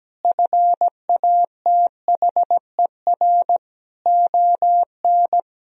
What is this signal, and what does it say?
Morse code, heard as a keyed tone, about 17 wpm, 700 Hz. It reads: FATHER ON
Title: Sentences from Top 500 Words and Abbreviations 17wpm